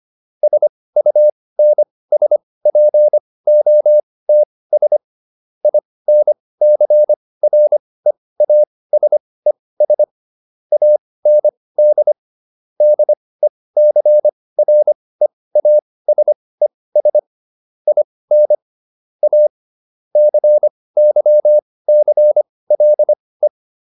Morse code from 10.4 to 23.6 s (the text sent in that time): AND DECREASES IN A CYCLE